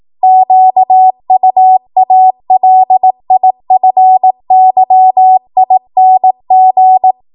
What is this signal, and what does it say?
Morse code, heard as a keyed tone, about 18 wpm, 750 Hz. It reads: QUALIFYING
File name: Archive / 18 WPM / 18WPM